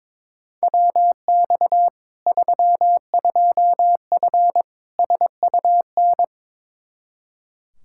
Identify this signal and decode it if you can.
Morse code, heard as a keyed tone, about 22 wpm, 700 Hz. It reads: WX 32F SUN
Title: Practice QSO - Exchange 2 - Info - 22wpm